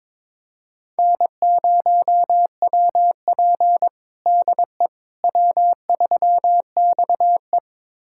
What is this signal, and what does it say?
Morse code, heard as a keyed tone, about 22 wpm, 700 Hz. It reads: N0WP DE W3XE
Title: USA Callsign DE Callsign - 4 Characters - 22wpm